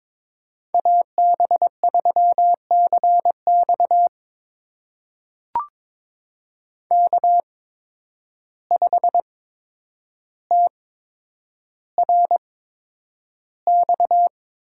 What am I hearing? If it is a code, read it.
Morse code, heard as a keyed tone, about 22 wpm, spaced as for 9 wpm, 700 Hz. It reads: AB3CX K5TRX
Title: USA Call Signs - 8x Character Spacing - 22wpm